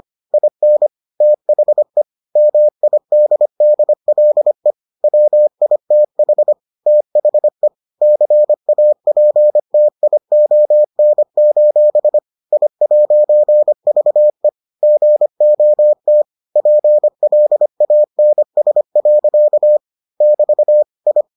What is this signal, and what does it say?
Morse code, heard as a keyed tone, about 25 wpm, 600 Hz. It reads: IN THE MIDDLE WITH THE CAPTION: I'VE GOT PLANS. <BT> I